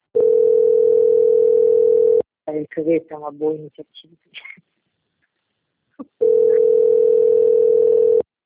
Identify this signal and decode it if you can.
Ringback tone (North America), 440 + 480 Hz, with speech.
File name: out- - - - - .19017